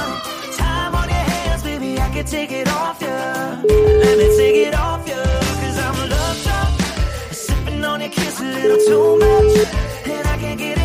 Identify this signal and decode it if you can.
Ringback tone (Europe, 425 Hz), with noise.